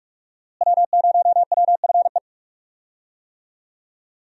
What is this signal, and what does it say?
Morse code, heard as a keyed tone, about 44 wpm, 700 Hz. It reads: W0WFE
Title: USA Call Signs - 5 Characters - Repeated 3x - 45wpm